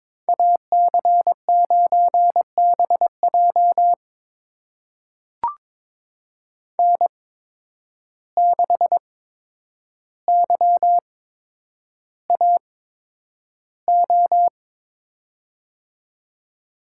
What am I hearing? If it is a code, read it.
Morse code, heard as a keyed tone, about 22 wpm, spaced as for 9 wpm, 700 Hz. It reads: AC9BJ N6YAO
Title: USA Call Signs - 5 Characters - 8x Character Spacing - 22wpm